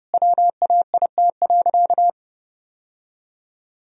Morse code, heard as a keyed tone, about 32 wpm, 700 Hz. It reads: WAIT.
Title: Sentences from Top 1000 Words - Review Part 4 - 30wpm